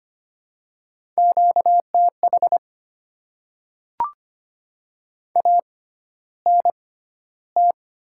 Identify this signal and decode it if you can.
Morse code, heard as a keyed tone, about 25 wpm, spaced as for 13 wpm, 700 Hz. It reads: QTH ANT
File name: Top 20 QSO Elements - 6x Character Spacing - 25wpm